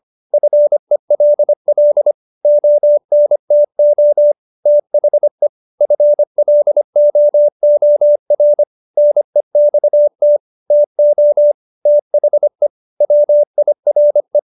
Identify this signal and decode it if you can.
Morse code, heard as a keyed tone, about 25 wpm, 590 Hz. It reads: FELL ONTO THE FLOOR NEXT TO THE WIRE